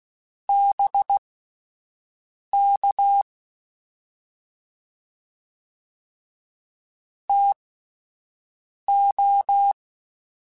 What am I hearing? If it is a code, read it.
Morse code, heard as a keyed tone, about 16 wpm, spaced as for 5 wpm, 800 Hz. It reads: BK TO
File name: (spaced) / QSOs Spaced-mp3s / WA7QOX 1 @10 sp